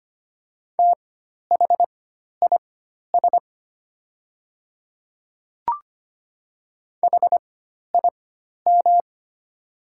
Morse code, heard as a keyed tone, about 25 wpm, spaced as for 12 wpm, 700 Hz. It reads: THIS HIM
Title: 4x Character Spacing - Top 100 Words - 25wpm